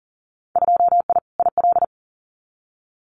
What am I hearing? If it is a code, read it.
Morse code, heard as a keyed tone, about 41 wpm, 700 Hz. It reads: 2I IL
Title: Winter Field Day Exchanges 40wpm